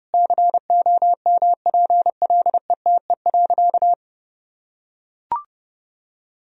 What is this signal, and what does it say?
Morse code, heard as a keyed tone, about 32 wpm, 700 Hz. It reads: COMPLETE.
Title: Sentences from Top 1000 Words - Review Part 2 - 30wpm